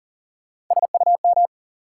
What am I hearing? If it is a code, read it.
Morse code, heard as a keyed tone, about 40 wpm, 700 Hz. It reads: SUM
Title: Sentences from Top 1000 Words - Review Part 6 - New Word Drill - 40wpm